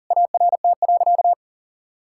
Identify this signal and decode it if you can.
Morse code, heard as a keyed tone, about 37 wpm, 700 Hz. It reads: ART.
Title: Sentences from Top 1000 Words - Review Part 6 - 40wpm